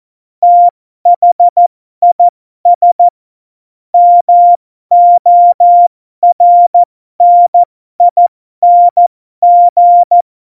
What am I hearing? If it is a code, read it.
Morse code, heard as a keyed tone, about 14 wpm, spaced as for 12 wpm, 700 Hz. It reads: THIS MORNING
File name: Morsekurs / Witze